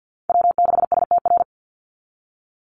Morse code, heard as a keyed tone, about 48 wpm, 700 Hz. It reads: W6STR
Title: NA SOTA Chaser Callsigns 50wpm